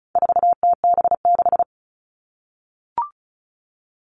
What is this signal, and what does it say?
Morse code, heard as a keyed tone, about 34 wpm, 700 Hz. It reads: 4TB6
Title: All Cut Numbers - 4-Digit 35wpm